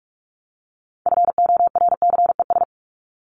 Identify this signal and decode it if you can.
Morse code, heard as a keyed tone, about 44 wpm, 700 Hz. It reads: FORCES
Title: ES:S Suffix - Top 100 Words with Base Words 45wpm